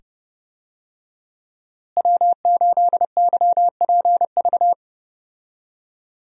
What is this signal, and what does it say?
Morse code, heard as a keyed tone, about 30 wpm, 700 Hz. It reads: W8YPV
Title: USA Call Signs - 5 Characters - Repeated 2x - 30wpm